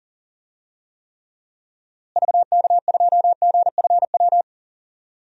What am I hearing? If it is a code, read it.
Morse code, heard as a keyed tone, about 37 wpm, 700 Hz. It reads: VK2GFW
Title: DXCC-100 Call Signs - 6 Characters - Repeated 2x - 40wpm